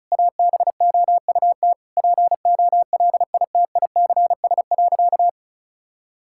Morse code, heard as a keyed tone, about 36 wpm, 700 Hz. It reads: ABOUT POLITICS.
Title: Sentences from Top 1000 Words - Review Part 6 - 35wpm